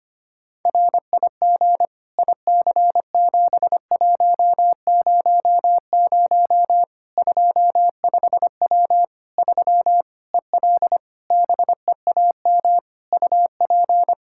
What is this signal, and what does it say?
Morse code, heard as a keyed tone, about 25 wpm, 700 Hz. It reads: RIG IC7100 25W 3 EL BEAM UP
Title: Practice QSO - Exchange 2 - Info - 25wpm